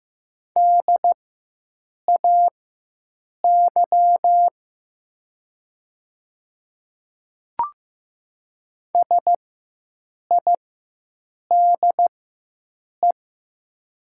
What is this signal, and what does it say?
Morse code, heard as a keyed tone, about 15 wpm, spaced as for 7 wpm, 700 Hz. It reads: DAY SIDE
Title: 4x Character Spacing - Top 500 Words - 15wpm